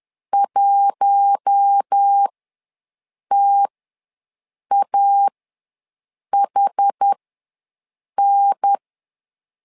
Morse code, heard as a keyed tone, about 10 wpm, spaced as for 6 wpm, 800 Hz. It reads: 1TAHN
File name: LEZIONI / 4 Q1 ws 10wpm 2538